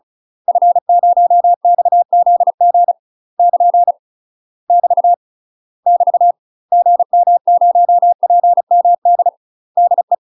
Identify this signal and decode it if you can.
Morse code, heard as a keyed tone, about 36 wpm, 700 Hz. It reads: F0XZG <KN> <BT> <BT> GM0PMD DE
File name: UK MASTER QSOs TRAINING / 35WPM / UK MASTER QSOs 0035